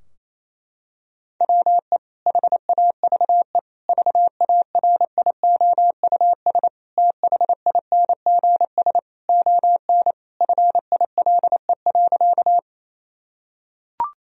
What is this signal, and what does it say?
Morse code, heard as a keyed tone, about 28 wpm, 700 Hz. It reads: WE HAVE VARIOUS THINGS ON FILE.
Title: Sentences from Top 1000 Words - Review Part 4 - 28wpm